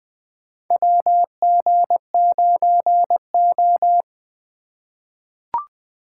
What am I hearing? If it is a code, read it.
Morse code, heard as a keyed tone, about 20 wpm, 700 Hz. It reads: WG9O
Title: USA Call Signs - 4 Characters - 6x Character Spacing - 20wpm